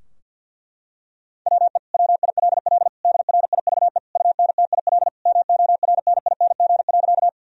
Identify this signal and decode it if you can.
Morse code, heard as a keyed tone, about 53 wpm, 700 Hz. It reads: WE WILL DRIVE UNTIL MORNING.
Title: Sentences from Top 1000 Words - Review Part 2 - 50wpm